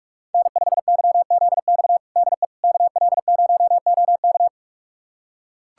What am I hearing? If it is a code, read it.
Morse code, heard as a keyed tone, about 42 wpm, 700 Hz. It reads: N5YZX DE KL0OK
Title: USA Callsign DE Callsign - 5 Characters - 45wpm